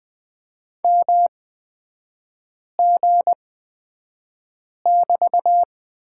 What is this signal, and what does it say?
Morse code, heard as a keyed tone, about 20 wpm, spaced as for 5 wpm, 700 Hz. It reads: MG<BT>
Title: 224 Lesson 36e - BT - Three Characters 20wpm